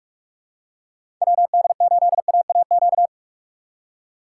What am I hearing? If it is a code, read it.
Morse code, heard as a keyed tone, about 46 wpm, 700 Hz. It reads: WD8AAQ
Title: USA Call Signs - 2x Character Spacing - 45wpm